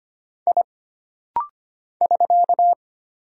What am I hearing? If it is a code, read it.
Morse code, heard as a keyed tone, about 25 wpm, 700 Hz. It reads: I <SK>
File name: Single Character (Letter:Number:Punctuation:Prosign) - Rapid-Fire - 25wpm